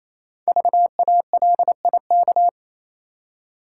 Morse code, heard as a keyed tone, about 27 wpm, 700 Hz. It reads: VALIK
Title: Code Groups - 5 Letters - 28wpm